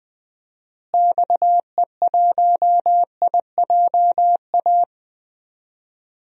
Morse code, heard as a keyed tone, about 21 wpm, 700 Hz. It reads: XE1IJA
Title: DXCC-100 Call Signs - 6 Characters - Repeated 2x - 20wpm